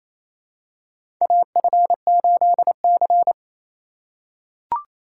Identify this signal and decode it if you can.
Morse code, heard as a keyed tone, about 28 wpm, 700 Hz. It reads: AF8C
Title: USA Call Signs - 4 Characters - 6x Character Spacing - 28wpm